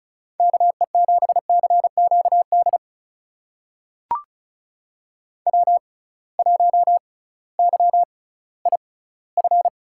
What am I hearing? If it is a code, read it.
Morse code, heard as a keyed tone, about 36 wpm, spaced as for 19 wpm, 700 Hz. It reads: KE7CQD W1YIF
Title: USA Call Signs - 6x Character Spacing - 35wpm